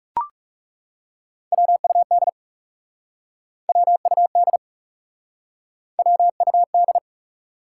Morse code, heard as a keyed tone, about 40 wpm, 700 Hz. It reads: WUD WUD WUD